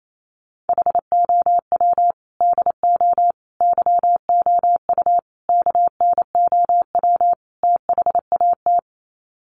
Morse code, heard as a keyed tone, about 28 wpm, 700 Hz. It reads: HOW DO YOU KNOW THAT
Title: Sets of 5 Words Encore 28wpm